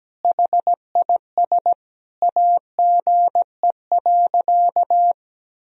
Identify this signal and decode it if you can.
Morse code, heard as a keyed tone, about 17 wpm, 700 Hz. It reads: HIS AGE.
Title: Sentences from Top 1000 Words - Review Part 3 - 17wpm